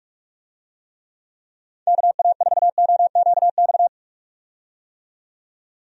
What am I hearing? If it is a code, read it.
Morse code, heard as a keyed tone, about 41 wpm, 700 Hz. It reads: KA4OQX
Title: USA Call Signs - 6 Characters - 45wpm